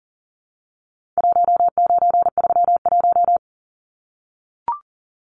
Morse code, heard as a keyed tone, about 39 wpm, 700 Hz. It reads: 1931
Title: Numbers - 4-Digit 40wpm